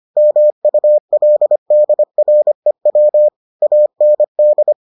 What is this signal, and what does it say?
Morse code, heard as a keyed tone, about 25 wpm, 600 Hz. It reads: MULDREW AND